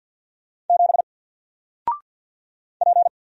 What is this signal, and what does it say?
Morse code, heard as a keyed tone, about 48 wpm, 700 Hz. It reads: 7 P
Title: Single Character (Letter:Number:Punctuation:Prosign) 50wpm